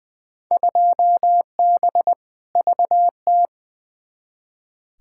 Morse code, heard as a keyed tone, about 20 wpm, 700 Hz. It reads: 2B VT
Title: Field Day Exchanges Encore - 20wpm